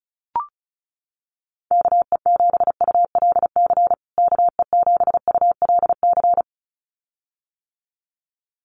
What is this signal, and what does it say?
Morse code, heard as a keyed tone, about 34 wpm, 700 Hz. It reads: KE7ULC KE7ULC
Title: USA Call Signs Repeated 2x 35wpm